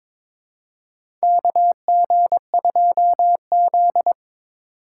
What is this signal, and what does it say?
Morse code, heard as a keyed tone, about 21 wpm, 700 Hz. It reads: KG2Z
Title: USA Call Signs - 4 Characters - 6x Character Spacing - 22wpm